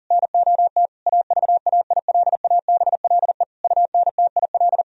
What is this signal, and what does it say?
Morse code, heard as a keyed tone, about 40 wpm, 700 Hz. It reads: NOT AVAILABLE UNTIL
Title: Sentences from Top 1000 Words - Review Part 3 - 40wpm